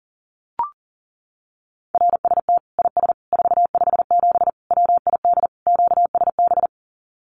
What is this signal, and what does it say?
Morse code, heard as a keyed tone, about 40 wpm, 700 Hz. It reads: RST IS 457 WID QSB
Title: Common QSO Phrases 40wpm